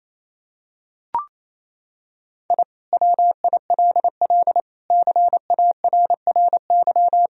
Morse code, heard as a keyed tone, about 27 wpm, 700 Hz. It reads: I WILL CARRY
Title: Sentences from Top 1000 Words - Review Part 1 - 28wpm